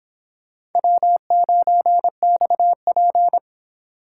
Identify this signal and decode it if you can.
Morse code, heard as a keyed tone, about 26 wpm, 700 Hz. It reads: W9XP